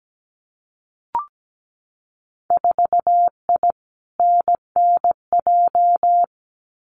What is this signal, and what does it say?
Morse code, heard as a keyed tone, about 17 wpm, 700 Hz. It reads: 4I NNJ
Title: Winter Field Day Exchanges 17wpm